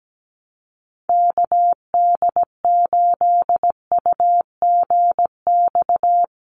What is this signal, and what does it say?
Morse code, heard as a keyed tone, about 17 wpm, 700 Hz. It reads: KD8UGX
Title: USA Call Signs 17wpm